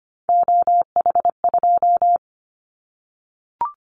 Morse code, heard as a keyed tone, about 25 wpm, 700 Hz. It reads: OH2
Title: Callsign Prefixes for Top 50 DXCC - 25wpm